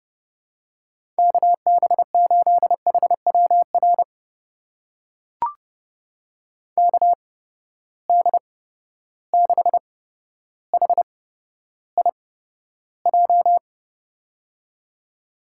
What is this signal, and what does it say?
Morse code, heard as a keyed tone, about 31 wpm, spaced as for 13 wpm, 700 Hz. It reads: KB8HWR KD6HIJ
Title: USA Call Signs - 6 Characters - 8x Character Spacing - 30wpm